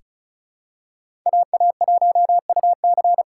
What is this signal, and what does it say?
Morse code, heard as a keyed tone, about 36 wpm, 700 Hz. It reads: AA1UC
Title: K1USN SST QSO Practice - 35wpm